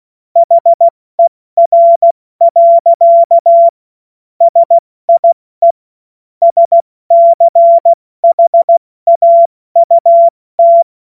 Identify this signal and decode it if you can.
Morse code, heard as a keyed tone, about 16 wpm, 680 Hz. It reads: HER. SIE SCHAUT
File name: Morsekurs / Witze / Witz 31. August 2021, 70 F80